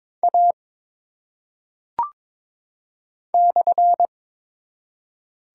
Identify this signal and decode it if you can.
Morse code, heard as a keyed tone, about 22 wpm, 700 Hz. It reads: A /
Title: Top 180 QSO Elements 22wpm